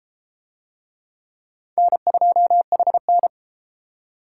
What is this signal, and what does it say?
Morse code, heard as a keyed tone, about 31 wpm, 700 Hz. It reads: N2HN